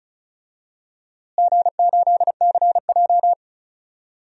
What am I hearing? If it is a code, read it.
Morse code, heard as a keyed tone, about 34 wpm, 700 Hz. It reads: G8CJ
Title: DXCC-100 Call Signs - 4 Characters - Repeated 3x - 35wpm